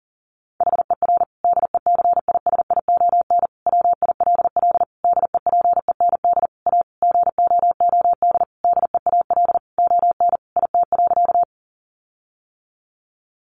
Morse code, heard as a keyed tone, about 40 wpm, 700 Hz. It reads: HER DECISION WILL DEPEND A GOOD DEAL ON IT.
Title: Sentences from Top 1300 Words 40wpm